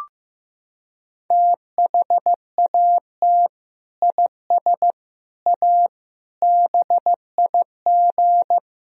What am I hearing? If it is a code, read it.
Morse code, heard as a keyed tone, about 15 wpm, 700 Hz. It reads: THAT IS A BIG
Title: Sentences from Top 1000 Words - Review Part 1 - 15wpm